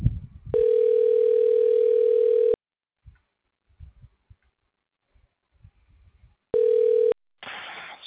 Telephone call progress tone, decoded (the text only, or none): ringback tone